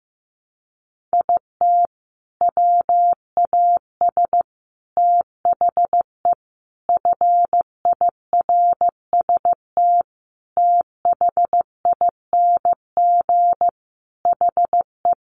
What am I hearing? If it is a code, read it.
Morse code, heard as a keyed tone, about 15 wpm, 700 Hz. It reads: IT WAS THE FIRST THING HE
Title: Sentences from Top 300 Words 15wpm